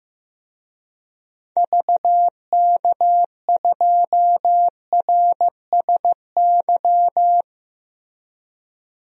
Morse code, heard as a keyed tone, about 15 wpm, 700 Hz. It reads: VK2RSY
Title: DXCC-100 Call Signs - 6 Characters - Repeated 2x - 15wpm